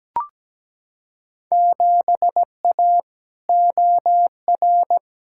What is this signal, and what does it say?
Morse code, heard as a keyed tone, about 17 wpm, 700 Hz. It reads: 7A OR
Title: Field Day Exchanges Encore - 17wpm